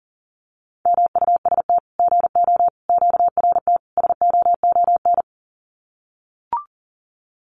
Morse code, heard as a keyed tone, about 40 wpm, 700 Hz. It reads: MUST GO QRT SOON
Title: Common QSO Phrases 40wpm